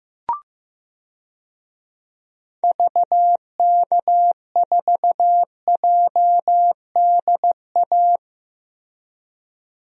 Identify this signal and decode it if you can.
Morse code, heard as a keyed tone, about 15 wpm, 700 Hz. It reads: VK4JDA
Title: DXCC-100 Call Signs - 6 Characters - Repeated 3x - 15wpm